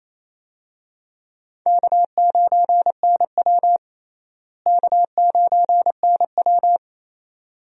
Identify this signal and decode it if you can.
Morse code, heard as a keyed tone, about 28 wpm, 700 Hz. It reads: K9NW K9NW
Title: USA Call Signs - 4 Characters - Repeated 2x - 28wpm